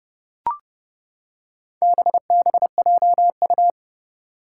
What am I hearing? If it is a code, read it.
Morse code, heard as a keyed tone, about 30 wpm, 690 Hz. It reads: BBJU
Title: Four Challenging Letter-Number Combinations 30wpm